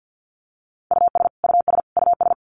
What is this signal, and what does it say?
Morse code, heard as a keyed tone, about 50 wpm, 700 Hz. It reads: US US US
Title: Top 200 Words Repeated 50wpm